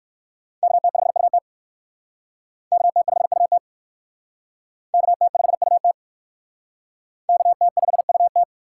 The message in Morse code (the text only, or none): KT5UT KT5UT KT5UT KT5UT